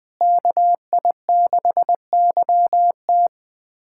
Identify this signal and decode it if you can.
Morse code, heard as a keyed tone, about 20 wpm, 700 Hz. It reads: KI6YT
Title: USA Callsign DE Callsign - 5 Characters - 20wpm